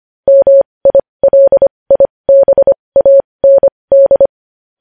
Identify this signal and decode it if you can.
Morse code, heard as a keyed tone, about 25 wpm, spaced as for 20 wpm, 550 Hz. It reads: MILIBAND
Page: News Headlines in Morse Code at 20 WPM